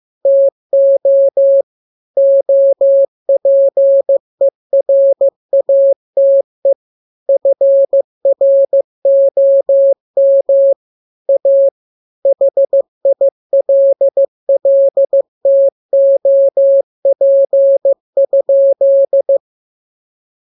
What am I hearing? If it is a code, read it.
Morse code, heard as a keyed tone, about 15 wpm, 550 Hz. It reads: TO OPERATE FROM A HILLTOP?